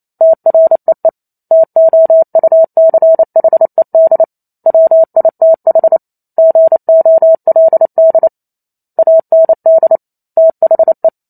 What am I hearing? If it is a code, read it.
Morse code, heard as a keyed tone, about 28 wpm, 650 Hz. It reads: TREE TOUCHED WITH GOLD AND THE